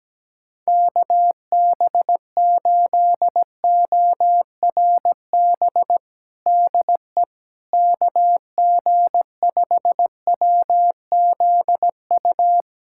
Morse code, heard as a keyed tone, about 17 wpm, 700 Hz. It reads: KB8ORB DE KG5WZU
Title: USA Callsign DE Callsign - 6 Characters - 17wpm